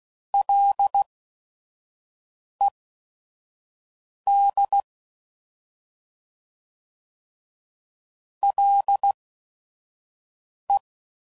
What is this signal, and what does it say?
Morse code, heard as a keyed tone, about 16 wpm, spaced as for 5 wpm, 800 Hz. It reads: LED LE
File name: Prefixes / FLs & THs / Single Fs Ls